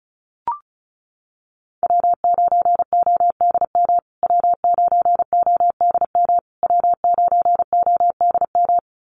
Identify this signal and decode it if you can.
Morse code, heard as a keyed tone, about 34 wpm, 700 Hz. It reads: W9ODM W9ODM W9ODM
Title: USA Call Signs Repeated 3x 35wpm